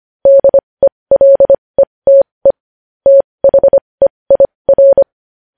Morse code, heard as a keyed tone, about 25 wpm, spaced as for 20 wpm, 550 Hz. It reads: DELETE THEIR
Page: News Headlines in Morse Code at 20 WPM